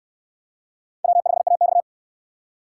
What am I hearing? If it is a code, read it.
Morse code, heard as a keyed tone, about 67 wpm, 700 Hz. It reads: W5AX